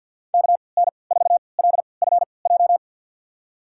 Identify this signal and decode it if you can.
Morse code, heard as a keyed tone, about 52 wpm, spaced as for 29 wpm, 700 Hz. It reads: KN4LFJ
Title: USA Call Signs - 6 Characters - 3x Character Spacing - 50wpm